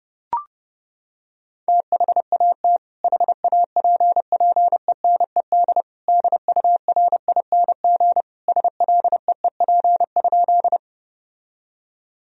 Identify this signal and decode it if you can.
Morse code, heard as a keyed tone, about 30 wpm, 700 Hz. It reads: THAT HAPPENED DURING SLEEP?
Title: Sentences from Top 1000 Words - Review Part 2 - 30wpm